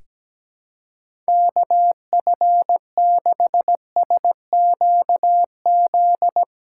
Morse code, heard as a keyed tone, about 17 wpm, 700 Hz. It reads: KF6SQZ
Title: USA Call Signs - 6 Characters - 4x Character Spacing - 17wpm